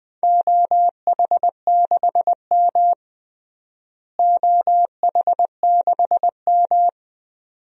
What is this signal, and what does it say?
Morse code, heard as a keyed tone, about 21 wpm, 700 Hz. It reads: OH6M OH6M
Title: DXCC-100 Call Signs - 4 Characters - Repeated 2x - 20wpm